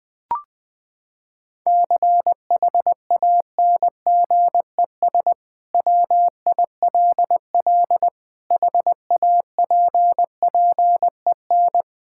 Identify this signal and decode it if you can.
Morse code, heard as a keyed tone, about 20 wpm, 700 Hz. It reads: CHANGES WILL HAPPEN
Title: Sentences with 5-Word Phrases - 20wpm